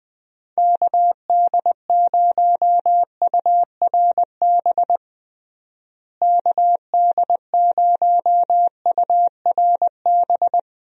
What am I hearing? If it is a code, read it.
Morse code, heard as a keyed tone, about 20 wpm, 700 Hz. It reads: KD0URB KD0URB
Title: USA Call Signs - 6 Characters - Repeated 3x - 20wpm